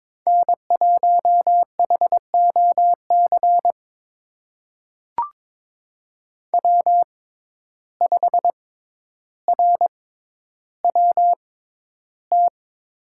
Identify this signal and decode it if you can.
Morse code, heard as a keyed tone, about 22 wpm, spaced as for 11 wpm, 700 Hz. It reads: N1HOC W5RWT
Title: USA Call Signs - 5 Characters - 6x Character Spacing - 22wpm